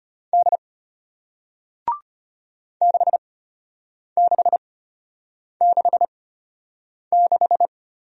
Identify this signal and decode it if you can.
Morse code, heard as a keyed tone, about 33 wpm, spaced as for 8 wpm, 700 Hz. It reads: D 6666